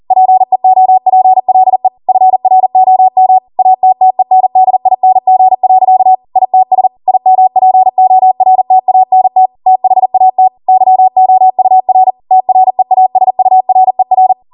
Morse code, heard as a keyed tone, about 40 wpm, 750 Hz. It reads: PEOPLE FROM ATTENDING. ITS IMPORTANT THAT YOUR TREASURER